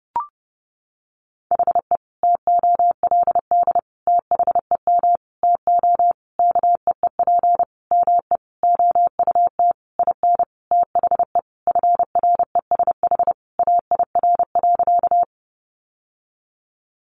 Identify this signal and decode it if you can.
Morse code, heard as a keyed tone, about 30 wpm, 700 Hz. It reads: HE TOLD THEM TO KEEP ME OUT IN THE FRESH AIR.
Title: Sentences from Top 1200 Words 30wpm